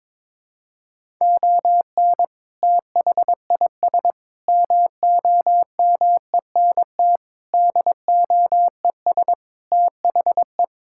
Morse code, heard as a keyed tone, about 22 wpm, 700 Hz. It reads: ON THIS MOMENT DOES THE
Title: Sentences from Top 1600 Words 22wpm